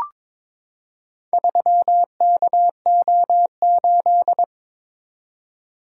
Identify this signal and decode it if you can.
Morse code, heard as a keyed tone, about 22 wpm, 700 Hz. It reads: 3KO8
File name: practice / 22wpm / Four Letter-Number Combinations - 22wpm